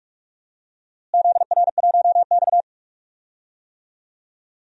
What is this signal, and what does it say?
Morse code, heard as a keyed tone, about 42 wpm, 700 Hz. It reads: ZR1X
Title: DXCC-100 Call Signs - 4 Characters - Repeated 3x - 45wpm